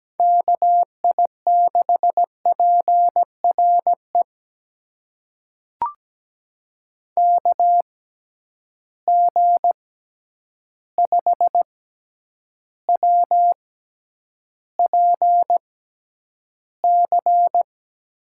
Morse code, heard as a keyed tone, about 17 wpm, spaced as for 9 wpm, 700 Hz. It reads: KI6PRE KG5WPC
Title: USA Call Signs - 6 Characters - 6x Character Spacing - 17wpm